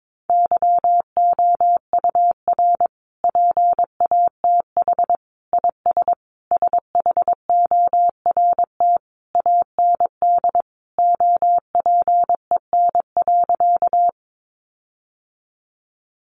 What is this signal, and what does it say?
Morse code, heard as a keyed tone, about 22 wpm, 700 Hz. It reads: YOUR PATH IS SHORT AND OPEN.
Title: Sentences from Top 1400 Words 22wpm